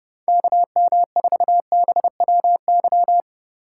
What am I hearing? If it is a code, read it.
Morse code, heard as a keyed tone, about 37 wpm, 700 Hz. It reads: KM4BWY
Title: Speed Racing - USA 2x3 Call Signs 30wpm